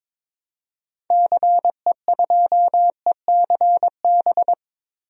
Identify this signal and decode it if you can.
Morse code, heard as a keyed tone, about 22 wpm, 700 Hz. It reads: CE2ECB
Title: DXCC-100 Call Signs - 6 Characters - Repeated 3x - 22wpm